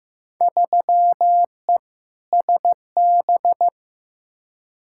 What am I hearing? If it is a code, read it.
Morse code, heard as a keyed tone, about 15 wpm, 700 Hz. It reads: 3E SB
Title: Field Day Exchanges Encore - 15wpm